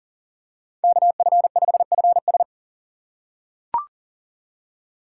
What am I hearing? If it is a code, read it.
Morse code, heard as a keyed tone, about 38 wpm, 700 Hz. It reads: KF5FS
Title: USA Call Signs - 5 Characters - 2x Character Spacing - 40wpm